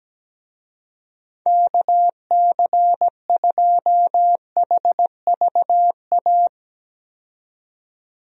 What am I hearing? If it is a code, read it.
Morse code, heard as a keyed tone, about 17 wpm, 700 Hz. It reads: KC2HVA
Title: USA Call Signs - 6 Characters - 17wpm